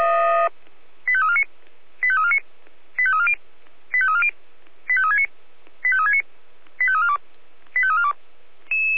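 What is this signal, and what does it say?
Five-tone selective call (ZVEI), heard as quick runs of five tones, tones of about 80 ms. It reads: 85349 85349 85340 85340 85369 85369 85342 85342
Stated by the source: ZVEI five-tone call received by SDR.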